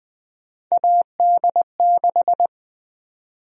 Morse code, heard as a keyed tone, about 20 wpm, 700 Hz. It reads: AD6
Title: USA Call Sign Prefixes - 20wpm